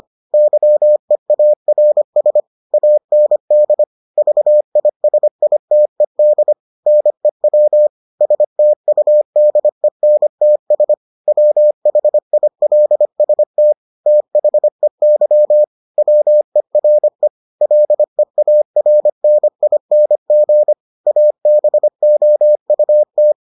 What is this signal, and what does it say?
Morse code, heard as a keyed tone, about 25 wpm, 600 Hz. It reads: YEARS AND VISITED NEW STUDENTS WHILST THEY WERE LEARNING ABOUT